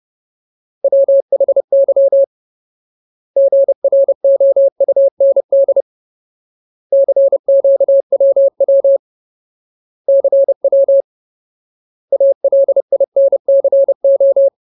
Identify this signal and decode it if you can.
Morse code, heard as a keyed tone, about 30 wpm, 550 Hz. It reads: WHY GROUND CQWW CW ALINCO
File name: Ultimate Word List / 30WPM Ultimate